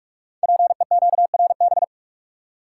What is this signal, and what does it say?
Morse code, heard as a keyed tone, about 44 wpm, 700 Hz. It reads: PEQRB
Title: Code Groups - 5 Letters - 45wpm